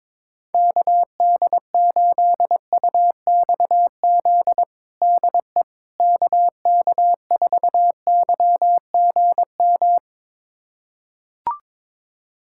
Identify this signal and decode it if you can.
Morse code, heard as a keyed tone, about 22 wpm, 700 Hz. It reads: KD8UXZ DE KK4YGM
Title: USA Callsign DE Callsign - 6 Characters - 22wpm